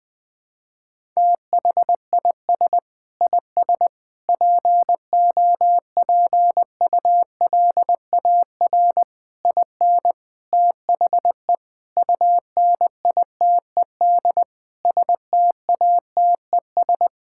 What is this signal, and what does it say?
Morse code, heard as a keyed tone, about 20 wpm, 700 Hz. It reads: THIS IS POPULAR IN THE UNITED STATES
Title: Sentences with 5-Word Phrases - 20wpm